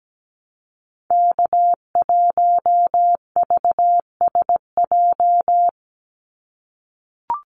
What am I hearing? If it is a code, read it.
Morse code, heard as a keyed tone, about 17 wpm, 700 Hz. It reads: K1VSJ
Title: Call Signs - 17wpm